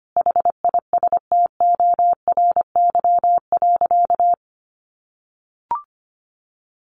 Morse code, heard as a keyed tone, about 25 wpm, 700 Hz. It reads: HISTORY.
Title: A Sound Of Thunder 25wpm